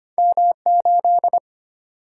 Morse code, heard as a keyed tone, about 25 wpm, 700 Hz. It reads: M8
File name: Two Letter-Number Combinations - 25wpm